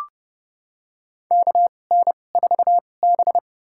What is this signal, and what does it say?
Morse code, heard as a keyed tone, about 30 wpm, spaced as for 22 wpm, 700 Hz. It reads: KN4B
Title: USA Call Signs - 4 Characters - 2x Character Spacing - 30wpm